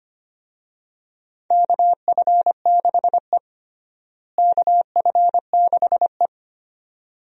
Morse code, heard as a keyed tone, about 25 wpm, 700 Hz. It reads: KF6E KF6E
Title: USA Call Signs - 4 Characters - Repeated 2x - 25wpm